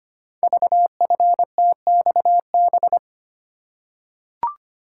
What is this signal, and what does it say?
Morse code, heard as a keyed tone, about 25 wpm, 700 Hz. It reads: VFTXB
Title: Code Groups - 5 Letters - 25wpm